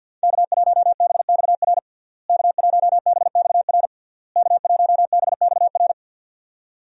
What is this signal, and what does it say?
Morse code, heard as a keyed tone, about 52 wpm, 700 Hz. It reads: K1BXR K1BXR K1BXR
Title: USA Call Signs - 5 Characters - Repeated 3x - 50wpm